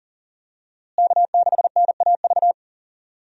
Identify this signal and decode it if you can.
Morse code, heard as a keyed tone, about 40 wpm, 700 Hz. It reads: K6NAV
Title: USA Call Signs - 5 Characters - Repeated 3x - 40wpm